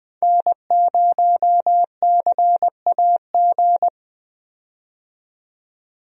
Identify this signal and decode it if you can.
Morse code, heard as a keyed tone, about 21 wpm, 700 Hz. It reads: N0CAG
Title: USA Call Signs - 5 Characters - 20wpm